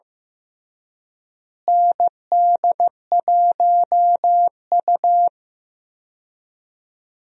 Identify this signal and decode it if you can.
Morse code, heard as a keyed tone, about 15 wpm, 700 Hz. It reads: ND1U
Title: USA Call Signs - 4 Characters - Repeated 2x - 15wpm